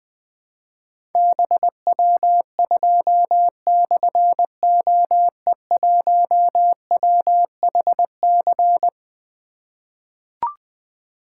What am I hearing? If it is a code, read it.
Morse code, heard as a keyed tone, about 20 wpm, 700 Hz. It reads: BW2/OE1WHC
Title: Difficult Call Signs 20wpm